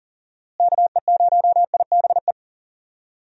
Morse code, heard as a keyed tone, about 40 wpm, 700 Hz. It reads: KE0IBE
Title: USA Call Signs - 6 Characters - Repeated 3x - 40wpm